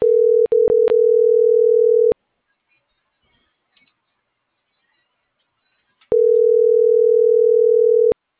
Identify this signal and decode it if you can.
Ringback tone (North America), 440 + 480 Hz.